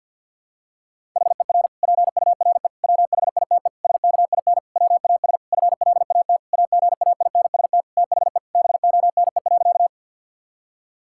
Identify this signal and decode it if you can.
Morse code, heard as a keyed tone, about 50 wpm, 700 Hz. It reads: HER PURE WHITE SKIN WAS FLAT AGAINST THE BONE.
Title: Sentences from Top 1600 Words 50wpm